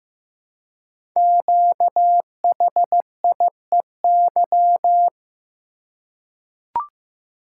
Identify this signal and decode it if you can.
Morse code, heard as a keyed tone, about 15 wpm, 700 Hz. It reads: QHIEY
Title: Code Groups - 5 Letters - 15wpm